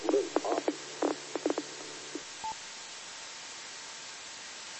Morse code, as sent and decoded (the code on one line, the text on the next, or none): none